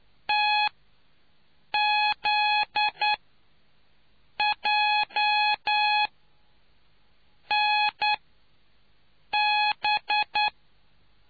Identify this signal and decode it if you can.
Morse code, heard as a keyed tone, about 9 wpm, spaced as for 5 wpm, 840 Hz. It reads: TZJNB